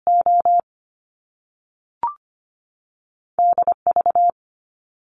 Morse code, heard as a keyed tone, about 25 wpm, 700 Hz. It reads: O D V
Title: Top Two Letter Combinations in Words 25wpm